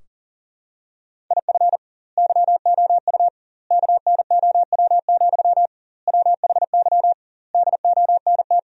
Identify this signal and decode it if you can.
Morse code, heard as a keyed tone, about 40 wpm, 700 Hz. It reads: IF YOU KNOW, WHY DONT
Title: 2x Word-Spacing - Sentences from Top 300 Words - 40wpm